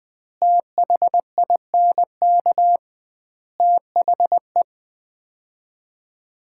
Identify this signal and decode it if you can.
Morse code, heard as a keyed tone, about 21 wpm, 700 Hz. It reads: THINK THE
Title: 2x Word-Spacing - Sets of 4 Words - 20wpm